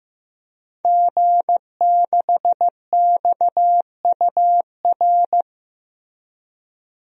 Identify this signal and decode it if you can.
Morse code, heard as a keyed tone, about 15 wpm, 700 Hz. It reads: G6XUR
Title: DXCC-100 Call Signs - 5 Characters - 15wpm